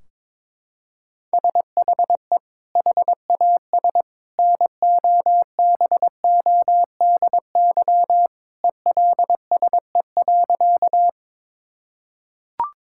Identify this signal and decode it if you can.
Morse code, heard as a keyed tone, about 21 wpm, 700 Hz. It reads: SHE HAS NOBODY ELSE.
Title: Sentences from Top 1600 Words 22wpm